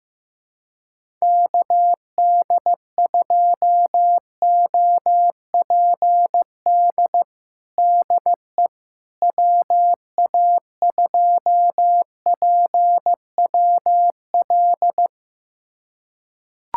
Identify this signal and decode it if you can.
Morse code, heard as a keyed tone, about 15 wpm, 700 Hz. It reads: KD2OPD DE WA2PWL
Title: USA Callsign DE Callsign 15wpm